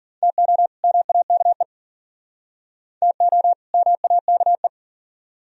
Morse code, heard as a keyed tone, about 53 wpm, 690 Hz. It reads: TO MAKE TO MAKE